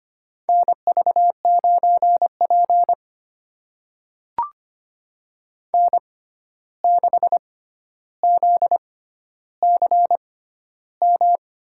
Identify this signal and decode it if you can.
Morse code, heard as a keyed tone, about 25 wpm, spaced as for 12 wpm, 700 Hz. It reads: NV9P N6ZCM
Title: USA Call Signs - 6x Character Spacing - 25wpm